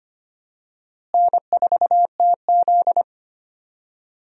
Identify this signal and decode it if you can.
Morse code, heard as a keyed tone, about 25 wpm, 700 Hz. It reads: N4TZ
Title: USA Call Signs - 4 Characters - Repeated 2x - 25wpm